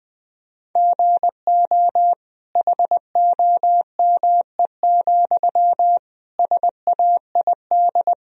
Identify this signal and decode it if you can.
Morse code, heard as a keyed tone, about 20 wpm, 700 Hz. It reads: GO HOME, SAID
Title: Sentences from Top 1000 Words - Review Part 1 - 20wpm